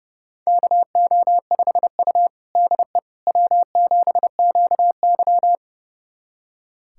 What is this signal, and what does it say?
Morse code, heard as a keyed tone, about 30 wpm, 700 Hz. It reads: KO5U DE W7QY
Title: USA Callsign DE Callsign - 4 Characters - 30wpm